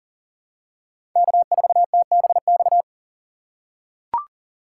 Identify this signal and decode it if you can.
Morse code, heard as a keyed tone, about 41 wpm, 700 Hz. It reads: K4TBX
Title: USA Call Signs - 5 Characters - 6x Character Spacing - 40wpm